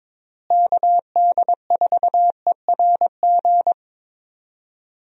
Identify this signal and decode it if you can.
Morse code, heard as a keyed tone, about 22 wpm, 700 Hz. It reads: KD4ERG
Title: USA Call Signs - 6 Characters - Repeated 2x - 22wpm